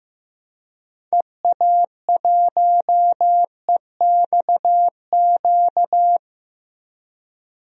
Morse code, heard as a keyed tone, about 15 wpm, 700 Hz. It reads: EA1EXQ
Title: DXCC-100 Call Signs - 6 Characters - Repeated 2x - 15wpm